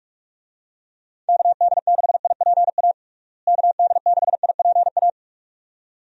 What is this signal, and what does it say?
Morse code, heard as a keyed tone, about 42 wpm, 700 Hz. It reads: KD6IPA KD6IPA
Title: USA Call Signs - 6 Characters - Repeated 2x - 45wpm